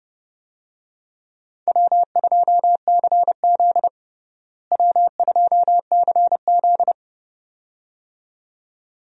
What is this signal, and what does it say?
Morse code, heard as a keyed tone, about 32 wpm, 700 Hz. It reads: W2CZ W2CZ
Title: USA Call Signs - 4 Characters - Repeated 2x - 30wpm